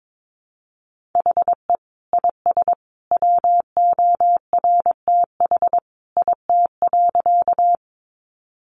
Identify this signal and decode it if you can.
Morse code, heard as a keyed tone, about 22 wpm, 700 Hz. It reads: HE IS WORTH IT.
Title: Sentences from Top 800 Words 22wpm